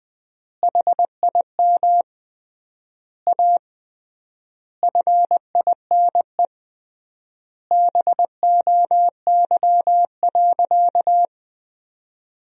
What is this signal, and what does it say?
Morse code, heard as a keyed tone, about 21 wpm, 700 Hz. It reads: HIM A FINE BOY.
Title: 3x Word-Spacing - Sentences from Top 500 Words - 20wpm